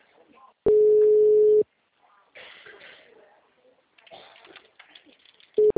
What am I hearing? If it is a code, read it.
Ringback tone (Europe, 425 Hz), with speech.
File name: out- -2006- - - .46059